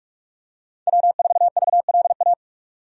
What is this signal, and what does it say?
Morse code, heard as a keyed tone, about 46 wpm, 700 Hz. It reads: W4VLA
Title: USA Call Signs - 5 Characters - 45wpm